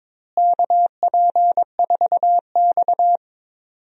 Morse code, heard as a keyed tone, about 22 wpm, 700 Hz. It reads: KP4X
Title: DXCC-100 Call Signs - 4 Characters - Repeated 3x - 22wpm